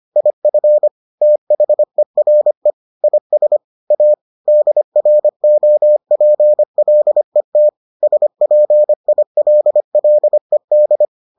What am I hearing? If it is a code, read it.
Morse code, heard as a keyed tone, about 25 wpm, 600 Hz. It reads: IF THERE IS A DROPLET SPILLED